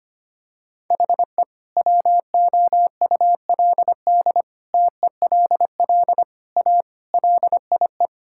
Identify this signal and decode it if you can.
Morse code, heard as a keyed tone, about 25 wpm, 700 Hz. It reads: HE WOULD TELL A LIE
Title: Sentences from Top 1600 Words 25wpm